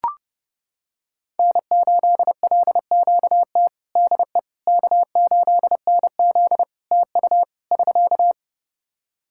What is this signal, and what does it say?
Morse code, heard as a keyed tone, about 30 wpm, 700 Hz. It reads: N8LQT DE K8NZ TU <SK>
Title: Practice QSO - Exchange 4 - Ending - 30wpm